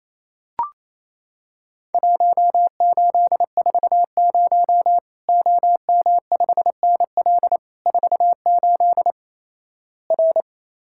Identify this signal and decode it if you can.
Morse code, heard as a keyed tone, about 27 wpm, 700 Hz. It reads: 1840 OM5NL 48 R
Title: Worked All Europe DX Contest - QTC Exchange - 28wpm